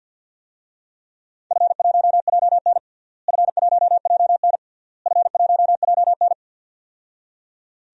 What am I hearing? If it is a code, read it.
Morse code, heard as a keyed tone, about 46 wpm, 700 Hz. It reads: F1JN F1JN F1JN
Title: DXCC-100 Call Signs - 4 Characters - Repeated 3x - 50wpm